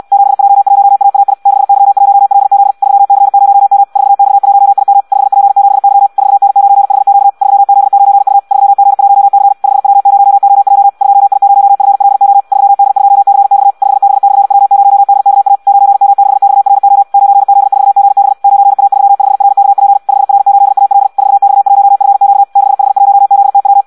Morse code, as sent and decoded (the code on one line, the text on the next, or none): --... .--- .---- .- .- .. / -.... -.-- .---- .-.. --.. / ...-- -.-. ----- -- / ....- -..- .---- .. -- / ..... .--. --... -.-- / ....- -. ----. .... --.- / ...-- --.. ----. ..- / ...-- --. .---- -.-. / ..... .-- ----- --- .--- / ..--- .. ----- ..-. .-.. --- / ...-- -.. ..--- -.-. -..- / ..... -... -..-. .-- ----- -.. -.- .- / ----. -- -.... -..- .-. --- / ----. --.. ....- --. -... / ----. .- ...-- ..... .-. -.- .--. / ..... .-. ---.. .- .-.. / ....- -..- .---- ..-. --.- / -.... .... .---- --.. - .--
7J1AAI 6Y1LZ 3C0M 4X1IM 5P7Y 4N9HQ 3Z9U 3G1C 5W0OJ 2I0FLO 3D2CX 5B/W0DKA 9M6XRO 9Z4GB 9A35RKP 5R8AL 4X1FQ 6H1ZTW